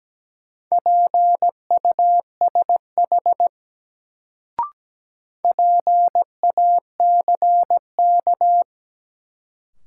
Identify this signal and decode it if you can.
Morse code, heard as a keyed tone, about 17 wpm, 700 Hz. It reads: PUSH PACK
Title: Sentences from Top 1300 Words - New Word Drill - 17wpm